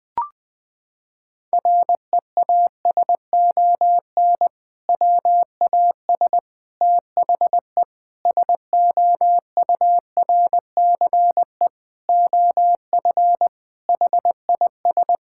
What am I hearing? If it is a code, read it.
Morse code, heard as a keyed tone, about 20 wpm, 700 Hz. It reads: REASON WAS THE SOURCE OF HIS
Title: Sentences from Top 1000 Words - Review Part 5 - 20wpm